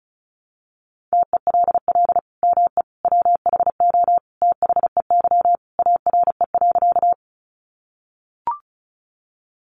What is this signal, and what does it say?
Morse code, heard as a keyed tone, about 34 wpm, 700 Hz. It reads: TELL ME WHO THEY ARE.
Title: Sentences from Top 700 Words 35wpm